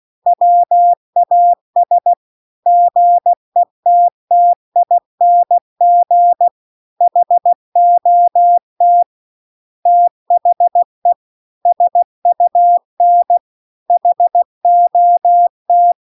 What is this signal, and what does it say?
Morse code, heard as a keyed tone, about 16 wpm, 700 Hz. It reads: WAS GETTING HOT THE SUN HOT
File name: CW-The-Big-Twoheaded-River-2-16WPM